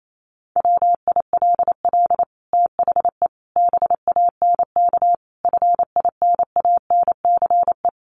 Morse code, heard as a keyed tone, about 28 wpm, 700 Hz. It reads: WILL THE BANK FINANCE
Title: Sentences from Top 1500 Words 28wpm